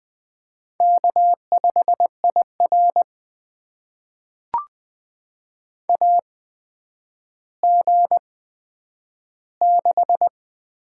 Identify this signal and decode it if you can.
Morse code, heard as a keyed tone, about 20 wpm, spaced as for 10 wpm, 700 Hz. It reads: K5IR AG6
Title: USA Call Signs - 8x Character Spacing - 20wpm